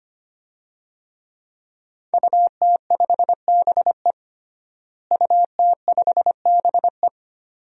Morse code, heard as a keyed tone, about 25 wpm, 700 Hz. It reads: UT5BE UT5BE
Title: DXCC-100 Call Signs - 5 Characters - Repeated 2x - 25wpm